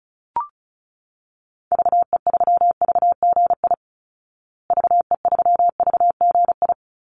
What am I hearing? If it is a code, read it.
Morse code, heard as a keyed tone, about 34 wpm, 700 Hz. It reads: VE3VGI VE3VGI
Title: Call Signs Repeated 2x - 35wpm